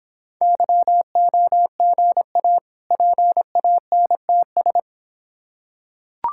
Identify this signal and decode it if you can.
Morse code, heard as a keyed tone, about 26 wpm, 700 Hz. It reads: YOGA PANTS